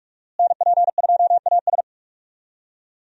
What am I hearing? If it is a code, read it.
Morse code, heard as a keyed tone, about 47 wpm, 700 Hz. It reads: NP2AS
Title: USA Call Signs - 5 Characters - Repeated 3x - 45wpm